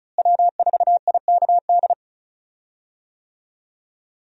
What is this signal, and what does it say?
Morse code, heard as a keyed tone, about 36 wpm, 700 Hz. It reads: W4IKD
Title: USA Call Signs - 5 Characters - Repeated 2x - 35wpm